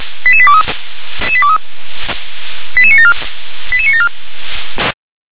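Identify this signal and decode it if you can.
Five-tone selective call (ZVEI), heard as quick runs of five tones, tones of about 70 ms. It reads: 88823 88974 88974
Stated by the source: ZVEI five-tone selective call 88774 received by SDR.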